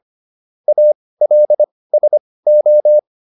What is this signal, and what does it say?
Morse code, heard as a keyed tone, about 25 wpm, spaced as for 18 wpm, 600 Hz. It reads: ALSO